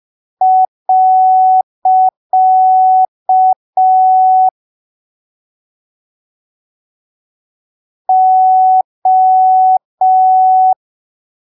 Morse code, heard as a keyed tone, about 5 wpm, 750 Hz. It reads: . O
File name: arquivos / texto / texto 20 05ppm